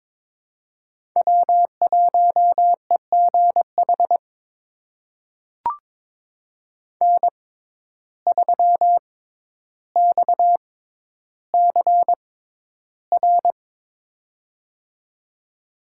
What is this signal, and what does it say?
Morse code, heard as a keyed tone, about 22 wpm, spaced as for 11 wpm, 700 Hz. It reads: W1EGH N3XCR
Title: USA Call Signs - 5 Characters - 6x Character Spacing - 22wpm